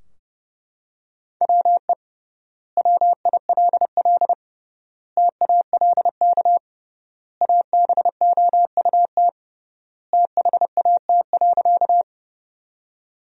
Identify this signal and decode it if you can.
Morse code, heard as a keyed tone, about 32 wpm, 700 Hz. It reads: WE WILL TALK ABOUT THAT.
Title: 3x Word-Spacing - Sentences from Top 400 Words - 30wpm